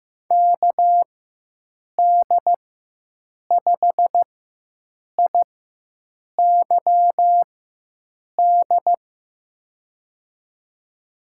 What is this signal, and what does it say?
Morse code, heard as a keyed tone, about 16 wpm, spaced as for 7 wpm, 700 Hz. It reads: KD5IYD
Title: USA Call Signs - 4x Character Spacing - 15wpm